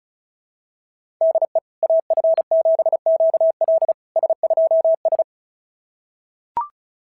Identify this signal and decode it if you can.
Morse code, heard as a keyed tone, about 33 wpm, 660 Hz. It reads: DE AF7QL S2S
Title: SOTA QSO Practice 35wpm